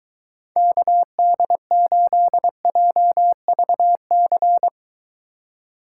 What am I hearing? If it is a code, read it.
Morse code, heard as a keyed tone, about 23 wpm, 700 Hz. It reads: KD8JVC